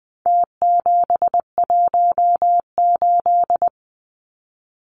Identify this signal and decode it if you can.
Morse code, heard as a keyed tone, about 20 wpm, 700 Hz. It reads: T718
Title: Numbers - Contest Style - 4-Digit 20wpm